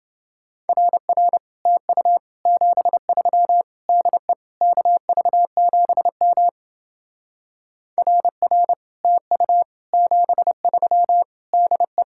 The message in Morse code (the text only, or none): RR TU 73 DE KV7M RR TU 73 DE